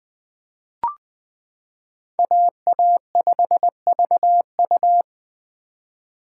Morse code, heard as a keyed tone, about 20 wpm, 700 Hz. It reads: AA5VU
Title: K1USN SST QSO Practice - 20wpm